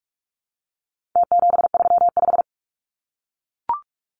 Morse code, heard as a keyed tone, about 44 wpm, 700 Hz. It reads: T735
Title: Numbers - Contest Style - 4-Digit 45wpm